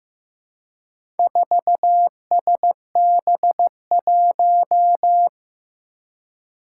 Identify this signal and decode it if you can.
Morse code, heard as a keyed tone, about 15 wpm, 700 Hz. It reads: 4SB1
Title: Four Challenging Letter-Number Combinations 15wpm